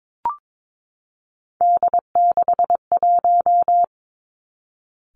Morse code, heard as a keyed tone, about 22 wpm, 700 Hz. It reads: D61
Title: Three Challenging Letter-Number Combinations 22wpm